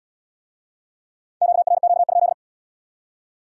Morse code, heard as a keyed tone, about 70 wpm, 700 Hz. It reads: QRQ?